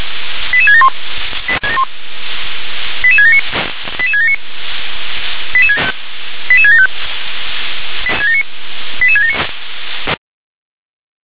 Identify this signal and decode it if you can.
Five-tone selective call (ZVEI), heard as quick runs of five tones, tones of about 70 ms. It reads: 88671 88679 88679 88675 88679
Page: SDR/ZVEI